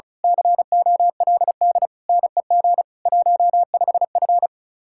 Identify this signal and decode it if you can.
Morse code, heard as a keyed tone, about 32 wpm, 700 Hz. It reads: COLD NEG 15F
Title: Speed-Racing Common QSO Phrases Encore - 30wpm